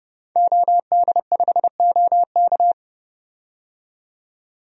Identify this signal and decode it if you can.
Morse code, heard as a keyed tone, about 28 wpm, 690 Hz. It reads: OD5OK
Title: DXCC-100 Call Signs - 5 Characters - 30wpm